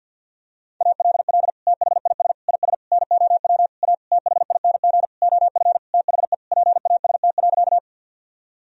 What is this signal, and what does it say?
Morse code, heard as a keyed tone, about 52 wpm, 700 Hz. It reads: ALL THIS IS NOW A THING OF THE PAST.
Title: Sentences from Top 1000 Words - Review Part 3 - 50wpm